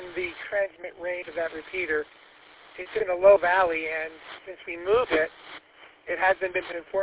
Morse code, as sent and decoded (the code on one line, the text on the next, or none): none